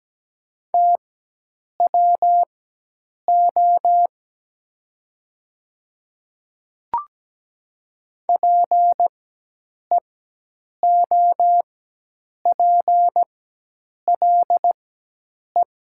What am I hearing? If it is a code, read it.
Morse code, heard as a keyed tone, about 17 wpm, spaced as for 8 wpm, 700 Hz. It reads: TWO PEOPLE
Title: 4x Character Spacing - Top 100 Words - 17wpm